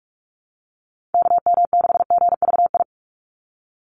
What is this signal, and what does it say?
Morse code, heard as a keyed tone, about 44 wpm, 700 Hz. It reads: KM6GVI
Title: USA Call Signs 45wpm